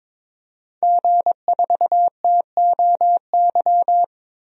Morse code, heard as a keyed tone, about 22 wpm, 700 Hz. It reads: G4TOY
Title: DXCC-100 Call Signs - 5 Characters - 22wpm